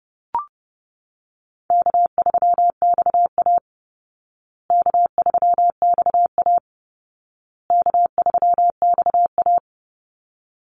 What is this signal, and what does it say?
Morse code, heard as a keyed tone, about 30 wpm, 700 Hz. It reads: K3XA K3XA K3XA
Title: Call Signs Repeated 2x - 30wpm